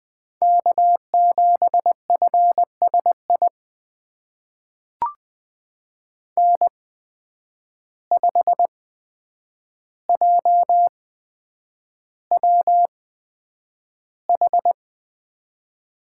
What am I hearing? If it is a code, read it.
Morse code, heard as a keyed tone, about 20 wpm, spaced as for 9 wpm, 700 Hz. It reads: K7FSI N5JWH
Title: USA Call Signs - 5 Characters - 8x Character Spacing - 20wpm